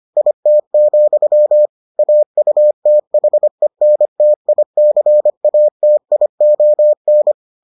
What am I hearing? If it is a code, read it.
Morse code, heard as a keyed tone, about 25 wpm, 600 Hz. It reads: IT, AUTHENTICATION